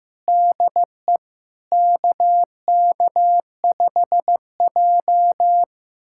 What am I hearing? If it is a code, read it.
Morse code, heard as a keyed tone, about 15 wpm, 700 Hz. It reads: DE KK5J
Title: USA Callsign DE Callsign - 4 Characters - 15wpm